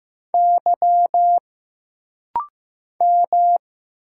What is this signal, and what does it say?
Morse code, heard as a keyed tone, about 15 wpm, 700 Hz. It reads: Y M
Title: Single Letter-Number Alt - Rapid-Fire 15wpm